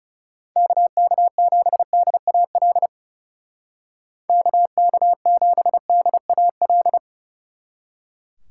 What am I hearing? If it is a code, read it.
Morse code, heard as a keyed tone, about 35 wpm, 700 Hz. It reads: KK7DAL KK7DAL
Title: Speed Racing - USA 2x3 Call Signs 30wpm